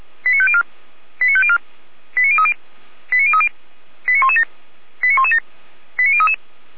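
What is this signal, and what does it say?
Five-tone selective call (ZVEI), heard as quick runs of five tones, tones of about 70 ms. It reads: 89504 89504 89030 89030 89117 89117 89033
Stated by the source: Received on SDR, ZVEI five-tone call 89080.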